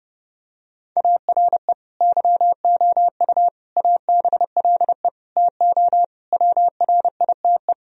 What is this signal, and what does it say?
Morse code, heard as a keyed tone, about 32 wpm, 700 Hz. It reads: ARE YOU ABLE TO WRITE
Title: Sentences from Top 1000 Words - Review Part 3 - 30wpm